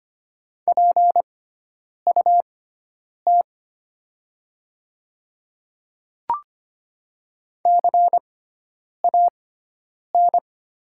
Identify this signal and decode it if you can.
Morse code, heard as a keyed tone, about 25 wpm, spaced as for 9 wpm, 700 Hz. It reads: PUT CAN
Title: 6x Character Spacing - Top 200 Words - 25wpm